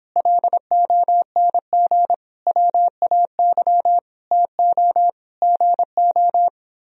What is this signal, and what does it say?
Morse code, heard as a keyed tone, about 29 wpm, 700 Hz. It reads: LONG WAY TO GO